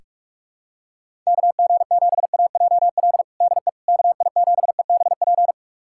Morse code, heard as a keyed tone, about 46 wpm, 700 Hz. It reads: KG7AJL DE KI7EBP
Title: USA Callsign DE Callsign - 6 Characters - 45wpm